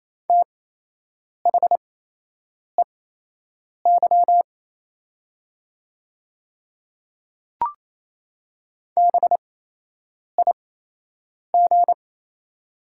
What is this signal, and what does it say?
Morse code, heard as a keyed tone, about 28 wpm, spaced as for 8 wpm, 700 Hz. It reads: THEY BIG
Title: 8x Character Spacing - Top 300 Words - 28wpm